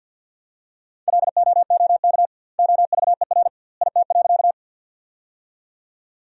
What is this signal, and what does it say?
Morse code, heard as a keyed tone, about 50 wpm, 700 Hz. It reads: LOOK OVER IT.
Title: 2x Word-Spacing - Sentences from Top 100 Words - 50wpm